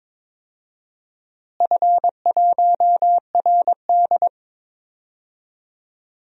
Morse code, heard as a keyed tone, about 22 wpm, 700 Hz. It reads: F1RD
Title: DXCC-100 Call Signs - 4 Characters - Repeated 3x - 22wpm